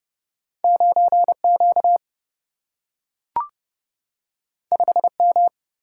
Morse code, heard as a keyed tone, about 30 wpm, 700 Hz. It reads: 9 Q 5 M
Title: Single-Letter Single-Number 30wpm